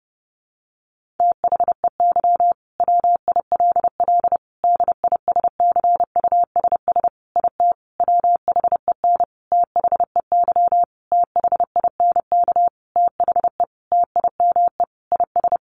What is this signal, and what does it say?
Morse code, heard as a keyed tone, about 30 wpm, 700 Hz. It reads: THEY WILL DISCUSS IT WHEN THEY THINK THE TIME IS
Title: Sentences from Top 1400 Words 30wpm